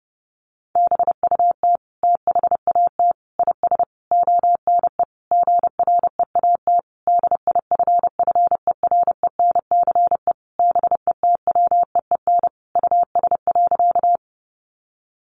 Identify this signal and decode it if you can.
Morse code, heard as a keyed tone, about 30 wpm, 700 Hz. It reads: BUT THAT IS ONE GREAT DIFFERENCE BETWEEN US.
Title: Sentences from Top 800 Words 30wpm